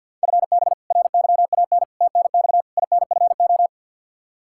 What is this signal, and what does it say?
Morse code, heard as a keyed tone, about 53 wpm, 700 Hz. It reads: FB RYAN TNX INFO
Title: Practice QSO - Exchange 3 - More Info - 50wpm